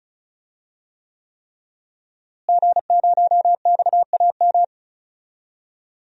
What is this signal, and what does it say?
Morse code, heard as a keyed tone, about 36 wpm, 700 Hz. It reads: G0XAM
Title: DXCC-100 Call Signs - 5 Characters - Repeated 3x - 35wpm